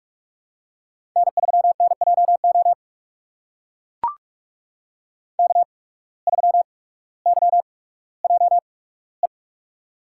Morse code, heard as a keyed tone, about 44 wpm, spaced as for 19 wpm, 700 Hz. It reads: N3NJO K3YJE
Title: USA Call Signs - 5 Characters - 8x Character Spacing - 45wpm